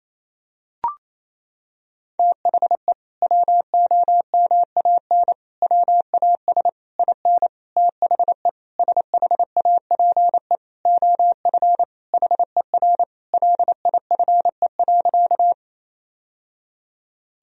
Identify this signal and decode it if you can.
Morse code, heard as a keyed tone, about 28 wpm, 700 Hz. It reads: THE WOMAN WAS IN THE SHAPE OF HER LIFE.
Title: Sentences from Top 1000 Words - Review Part 3 - 28wpm